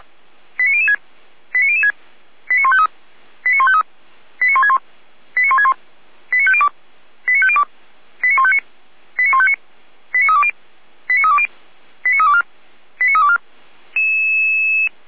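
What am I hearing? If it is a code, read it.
Five-tone selective call (ZVEI), heard as quick runs of five tones, tones of about 70 ms. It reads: 89006 89006 89153 89153 89161 89161 89502 89502 89159 89159 89320 89320 89324 89324
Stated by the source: ZVEI five-tone selective call 89099, received on SDR.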